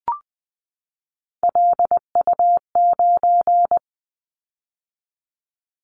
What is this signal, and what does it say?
Morse code, heard as a keyed tone, about 20 wpm, 700 Hz. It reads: LU9
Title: Callsign Prefixes for Top 50 DXCC - 20wpm